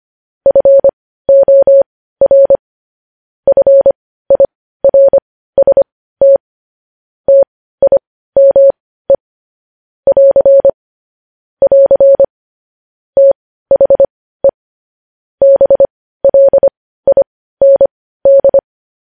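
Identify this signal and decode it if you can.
Morse code, heard as a keyed tone, about 25 wpm, spaced as for 15 wpm, 550 Hz. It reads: FOR FIRST TIME <AR> <AR> THE BLIND
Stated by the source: Morse code news headlines practice at 15.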